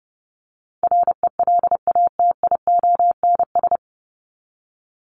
Morse code, heard as a keyed tone, about 30 wpm, 700 Hz. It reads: RELATIONS
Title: ES:S Suffix - Top 100 Words with Base Words 30wpm